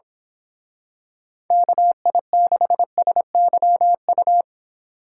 Morse code, heard as a keyed tone, about 26 wpm, 700 Hz. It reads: KI6SYU
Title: Speed Racing - USA 2x3 Call Signs 22wpm